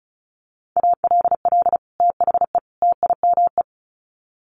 Morse code, heard as a keyed tone, about 34 wpm, 700 Hz. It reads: ALL THE TIME
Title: Sets of 3 Words 35wpm